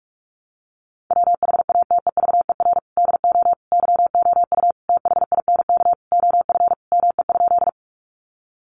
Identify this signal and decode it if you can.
Morse code, heard as a keyed tone, about 44 wpm, 700 Hz. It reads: WHATEVER DO YOU THINK OF ME?
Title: Sentences from Top 1100 Words 45wpm